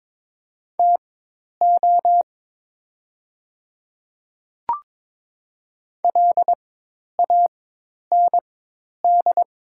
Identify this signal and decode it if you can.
Morse code, heard as a keyed tone, about 23 wpm, spaced as for 10 wpm, 690 Hz. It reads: TO LAND
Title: 4x Character Spacing - Top 200 Words - 22wpm